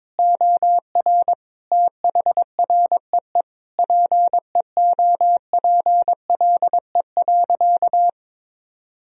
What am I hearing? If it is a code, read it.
Morse code, heard as a keyed tone, about 22 wpm, 700 Hz. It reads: OR THREE PEOPLE.
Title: Sentences from Top 1000 Words - Review Part 1 - 22wpm